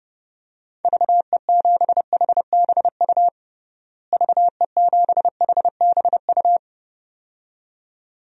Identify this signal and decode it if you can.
Morse code, heard as a keyed tone, about 30 wpm, 700 Hz. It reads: VE7HBU VE7HBU
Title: DXCC-100 Call Signs - 6 Characters - Repeated 2x - 30wpm